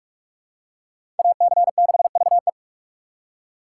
Morse code, heard as a keyed tone, about 44 wpm, 700 Hz. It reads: AC6VE